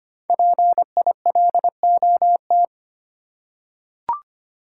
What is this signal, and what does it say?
Morse code, heard as a keyed tone, about 25 wpm, 700 Hz. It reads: PILOT
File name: 5 Letter Words - First 3 Letters Given - 25wpm